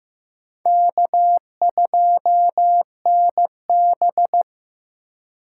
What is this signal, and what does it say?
Morse code, heard as a keyed tone, about 18 wpm, 700 Hz. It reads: K2NB
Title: Speed Racing - USA 1x2 Call Signs 15wpm